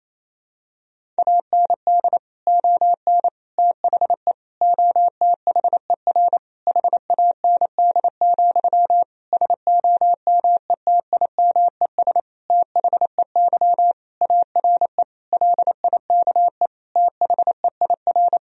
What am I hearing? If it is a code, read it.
Morse code, heard as a keyed tone, about 28 wpm, 700 Hz. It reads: AND ON THE OTHER HAND, SOMETIMES THEY ARE LIKE THEIR
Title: Sentences from Top 1000 Words - Review Part 3 - 28wpm